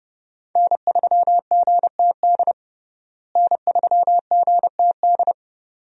Morse code, heard as a keyed tone, about 30 wpm, 700 Hz. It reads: N3GTD N3GTD
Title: USA Call Signs - 5 Characters - Repeated 2x - 30wpm